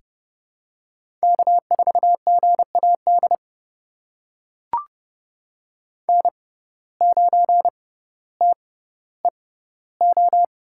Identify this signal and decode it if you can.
Morse code, heard as a keyed tone, about 30 wpm, spaced as for 15 wpm, 690 Hz. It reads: K4GAD N9TEO
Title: USA Call Signs - 5 Characters - 6x Character Spacing - 30wpm